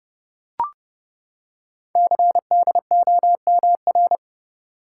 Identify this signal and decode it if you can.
Morse code, heard as a keyed tone, about 30 wpm, 700 Hz. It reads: CDOMR
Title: Code Groups - 5 Letters - 30wpm